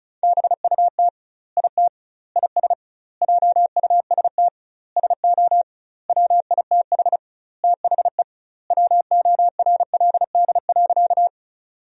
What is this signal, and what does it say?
Morse code, heard as a keyed tone, about 34 wpm, 700 Hz. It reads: BUT IT IS JUST SO WITH THE WORLD.
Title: 2x Word-Spacing - Sentences from Top 200 Words - 35wpm